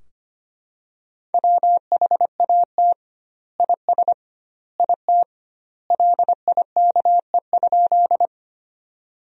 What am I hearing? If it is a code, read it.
Morse code, heard as a keyed tone, about 25 wpm, 700 Hz. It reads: WHAT IS IT LIKE?
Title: 2x Word-Spacing - Sentences from Top 100 Words - 25wpm